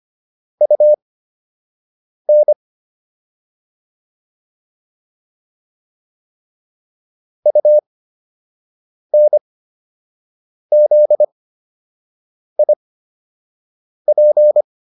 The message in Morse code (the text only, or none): UN UNZIP